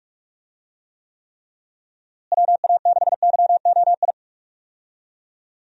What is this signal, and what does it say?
Morse code, heard as a keyed tone, about 43 wpm, 700 Hz. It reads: WA6YOI
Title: USA Call Signs - 6 Characters - Repeated 3x - 45wpm